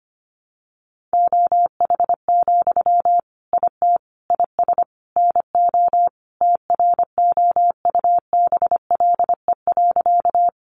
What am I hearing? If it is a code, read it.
Morse code, heard as a keyed tone, about 25 wpm, 700 Hz. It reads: OH, IT IS NO TROUBLE.
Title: Sentences from Top 1200 Words 25wpm